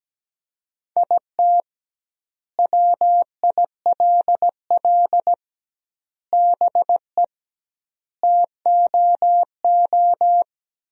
Morse code, heard as a keyed tone, about 17 wpm, 700 Hz. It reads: IT WILL BE TOO
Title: 2x Word-Spacing - Sentences from Top 300 Words - 17wpm